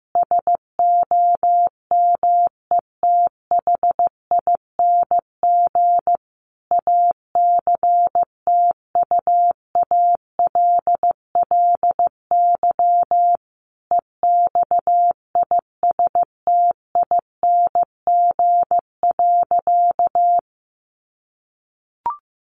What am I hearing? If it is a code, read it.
Morse code, heard as a keyed tone, about 15 wpm, 700 Hz. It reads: SOMETHING ACTUALLY EXISTING.
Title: Sentences from Top 1500 Words 15wpm